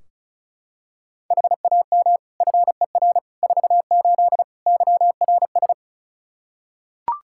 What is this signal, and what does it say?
Morse code, heard as a keyed tone, about 34 wpm, 700 Hz. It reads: HAM FER 48 YRS
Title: K1USN SST QSO Practice 35wpm